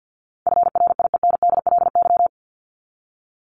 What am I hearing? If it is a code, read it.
Morse code, heard as a keyed tone, about 50 wpm, 700 Hz. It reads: FRIENDLY
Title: LY Suffix - Top 100 Words with Base Words 50wpm